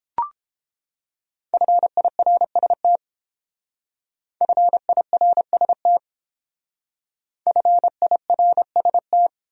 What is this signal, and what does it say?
Morse code, heard as a keyed tone, about 30 wpm, 700 Hz. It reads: FIRST FIRST FIRST